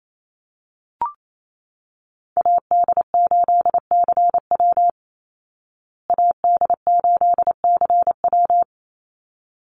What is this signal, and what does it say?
Morse code, heard as a keyed tone, about 28 wpm, 700 Hz. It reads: AD8CW AD8CW
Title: Call Signs Repeated 3x - 28wpm